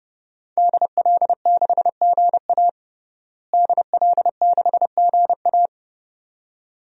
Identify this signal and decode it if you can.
Morse code, heard as a keyed tone, about 30 wpm, 700 Hz. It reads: DL6GA DL6GA
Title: DXCC-100 Call Signs - 5 Characters - Repeated 2x - 30wpm